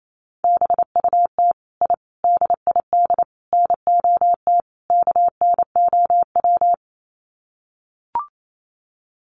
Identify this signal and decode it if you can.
Morse code, heard as a keyed tone, about 28 wpm, 700 Hz. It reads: BUT I DID NOT KNOW
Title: Sets of 5 Words Encore 28wpm